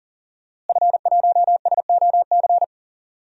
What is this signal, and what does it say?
Morse code, heard as a keyed tone, about 39 wpm, 700 Hz. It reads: F1SOC
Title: DXCC-100 Call Signs - 5 Characters - Repeated 2x - 40wpm